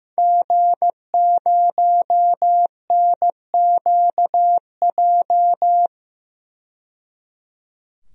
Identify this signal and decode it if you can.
Morse code, heard as a keyed tone, about 15 wpm, 700 Hz. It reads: G0NQJ
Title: DXCC-100 Call Signs - 5 Characters - 15wpm